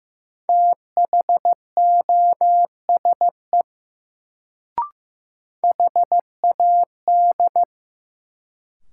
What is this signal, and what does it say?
Morse code, heard as a keyed tone, about 15 wpm, 700 Hz. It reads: THOSE HAD
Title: Sets of 2 Words - New Word Drill - 15wpm